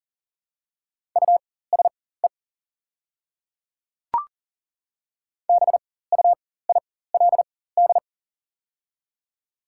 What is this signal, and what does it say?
Morse code, heard as a keyed tone, about 40 wpm, spaced as for 19 wpm, 700 Hz. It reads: USE BUILD
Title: 4x Character Spacing - Top 200 Words - 40wpm